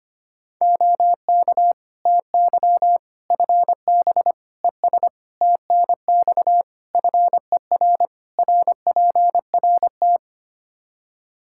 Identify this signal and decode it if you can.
Morse code, heard as a keyed tone, about 25 wpm, 700 Hz. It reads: OK TY FB ES TNX FER RPRT
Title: Practice QSO - Exchange 2 - Info - 25wpm